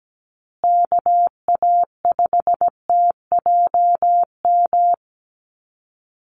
Morse code, heard as a keyed tone, about 17 wpm, 700 Hz. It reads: KA5TJM
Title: USA Call Signs Repeated 3x 17wpm